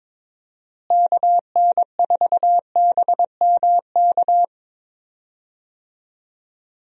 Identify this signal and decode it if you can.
Morse code, heard as a keyed tone, about 21 wpm, 700 Hz. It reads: KN4BMK
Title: USA Call Signs - 6 Characters - Repeated 3x - 22wpm